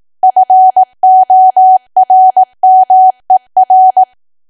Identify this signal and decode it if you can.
Morse code, heard as a keyed tone, about 18 wpm, 750 Hz. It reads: FORMER
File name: Archive / 18 WPM / 18WPM